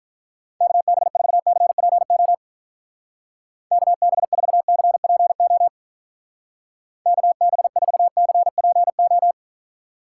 Morse code, heard as a keyed tone, about 43 wpm, 700 Hz. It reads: KB4CPO KB4CPO KB4CPO
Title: Speed Racing - USA 2x3 Call Signs 35wpm